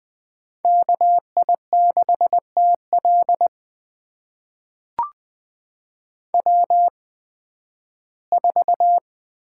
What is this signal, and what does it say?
Morse code, heard as a keyed tone, about 20 wpm, spaced as for 13 wpm, 700 Hz. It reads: KI6TL W4
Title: USA Call Signs - 8x Character Spacing - 20wpm